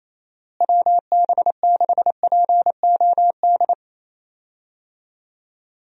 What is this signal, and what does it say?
Morse code, heard as a keyed tone, about 28 wpm, 700 Hz. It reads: WB6POD
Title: USA Call Signs - 6 Characters - Repeated 3x - 28wpm